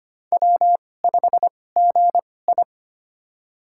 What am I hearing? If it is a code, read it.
Morse code, heard as a keyed tone, about 25 wpm, spaced as for 18 wpm, 700 Hz. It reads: W5GI
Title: USA Call Signs - 4 Characters - 2x Character Spacing - 25wpm